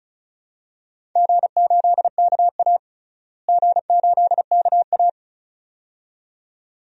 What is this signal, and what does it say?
Morse code, heard as a keyed tone, about 36 wpm, 700 Hz. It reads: G8KA G8KA
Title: DXCC-100 Call Signs - 4 Characters - Repeated 2x - 35wpm